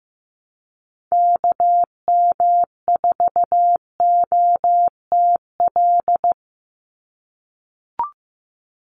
Morse code, heard as a keyed tone, about 15 wpm, 700 Hz. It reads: KM4OTL
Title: USA Call Signs 15wpm